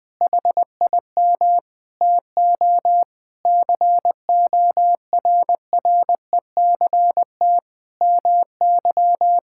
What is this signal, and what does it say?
Morse code, heard as a keyed tone, about 20 wpm, 700 Hz. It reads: HIM TO CORRECT MY
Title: Sentences from Top 1000 Words - Review Part 5 - 20wpm